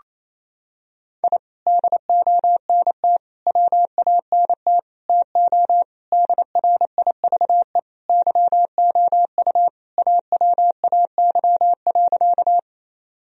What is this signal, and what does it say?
Morse code, heard as a keyed tone, about 28 wpm, 700 Hz. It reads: I DONT WANT TO DRIVE YOU AWAY.
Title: Sentences from Top 1000 Words - Review Part 2 - 28wpm